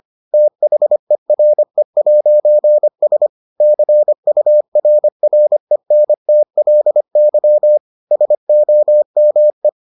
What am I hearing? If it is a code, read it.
Morse code, heard as a keyed tone, about 25 wpm, 600 Hz. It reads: THERE'S CURRENTLY SOME